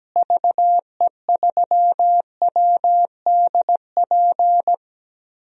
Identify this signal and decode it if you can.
Morse code, heard as a keyed tone, about 17 wpm, 700 Hz. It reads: VE3WDP
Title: DXCC-100 Call Signs - 6 Characters - Repeated 3x - 17wpm